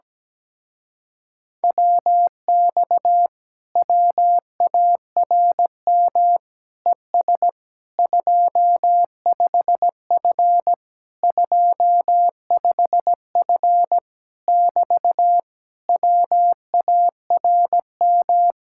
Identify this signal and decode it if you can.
Morse code, heard as a keyed tone, about 17 wpm, 700 Hz. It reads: WX WARM ES 25F 25F <BT> WARM
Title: Speed-Racing Common QSO Phrases - 17wpm